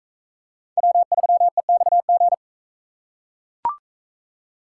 Morse code, heard as a keyed tone, about 39 wpm, 700 Hz. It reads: W3EXG